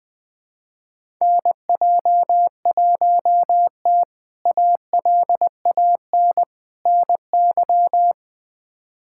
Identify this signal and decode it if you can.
Morse code, heard as a keyed tone, about 20 wpm, 700 Hz. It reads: NJ1T ALAN NY
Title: K1USN SST QSO Practice - 20wpm